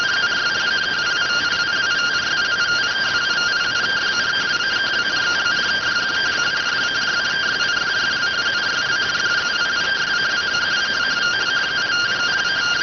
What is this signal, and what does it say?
Two-tone teleprinter signal, about 170 Hz shift.